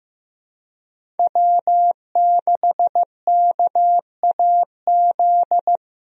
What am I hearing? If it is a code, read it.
Morse code, heard as a keyed tone, about 15 wpm, 700 Hz. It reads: W6KAZ
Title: USA Call Signs - 5 Characters - Repeated 3x - 15wpm